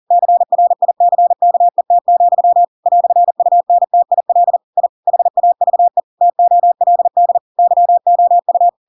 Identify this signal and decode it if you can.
Morse code, heard as a keyed tone, about 40 wpm, 700 Hz. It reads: CRICKET, "UNTIL I HAVE TOLD YOU
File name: Pinocchio / Pinocchio ch4 40 40 ik1hge